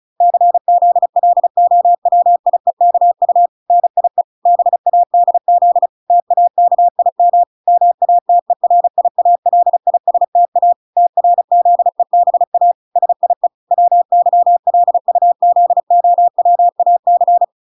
Morse code, heard as a keyed tone, about 34 wpm, 700 Hz. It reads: CZLOWIEKU NIE BADZ TAKIM MATERIALISTA TRZEBA SIE WYLUZOWAC